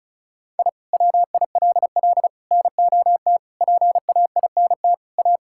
Morse code, heard as a keyed tone, about 36 wpm, 700 Hz. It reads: I WILL NOT PAINT A
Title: Sentences from Top 1000 Words - Review Part 6 - 35wpm